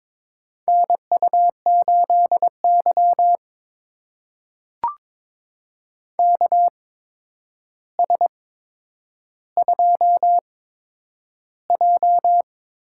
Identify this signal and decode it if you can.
Morse code, heard as a keyed tone, about 22 wpm, spaced as for 9 wpm, 700 Hz. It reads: NU8Y KS2J
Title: USA Call Signs - 4 Characters - 8x Character Spacing - 22wpm